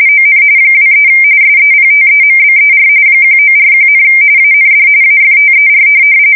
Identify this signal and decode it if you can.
RTTY, about 45.45 baud. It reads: ON 23 SEPTEMBER 1941, AND AFTER